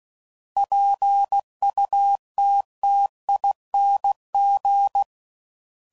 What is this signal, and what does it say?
Morse code, heard as a keyed tone, about 16 wpm, 800 Hz. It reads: PUTTING
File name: Suffixes / Special INGs / Sngl Ings